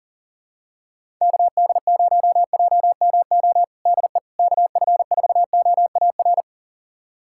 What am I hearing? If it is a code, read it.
Morse code, heard as a keyed tone, about 40 wpm, 700 Hz. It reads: KD0JMO DE KF4OAR
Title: USA Callsign DE Callsign - 6 Characters - 40wpm